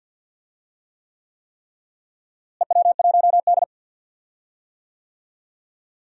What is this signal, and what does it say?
Morse code, heard as a keyed tone, about 46 wpm, 700 Hz. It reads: EW1D
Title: DXCC-100 Call Signs - 4 Characters - Repeated 3x - 50wpm